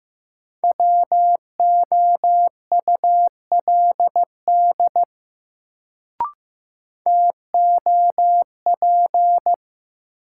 Morse code, heard as a keyed tone, about 15 wpm, 700 Hz. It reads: WOULD TOP
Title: Sets of 2 Words - New Word Drill - 15wpm